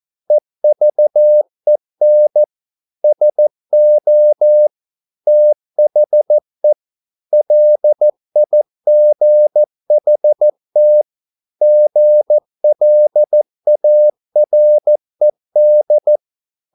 Morse code, heard as a keyed tone, about 14 wpm, 600 Hz. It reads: EVEN SO THE LIGHT GLARED